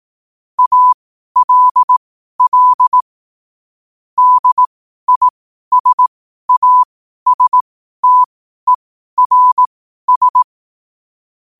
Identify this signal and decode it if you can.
Morse code, heard as a keyed tone, about 18 wpm, spaced as for 12 wpm, 1000 Hz. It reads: ALL DISASTERS
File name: qotdfiles / 12wpm